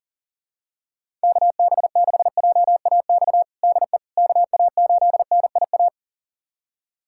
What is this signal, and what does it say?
Morse code, heard as a keyed tone, about 40 wpm, 700 Hz. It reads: KB6JAX DE KA8NIA
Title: USA Callsign DE Callsign - 6 Characters - 40wpm